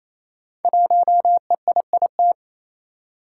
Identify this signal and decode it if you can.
Morse code, heard as a keyed tone, about 27 wpm, 700 Hz. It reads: 1EIIT
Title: Five Challenging Letter-Number Combinations 28wpm